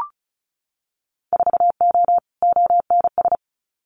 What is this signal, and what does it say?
Morse code, heard as a keyed tone, about 34 wpm, 700 Hz. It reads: 4O ONS
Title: Winter Field Day Exchanges 35wpm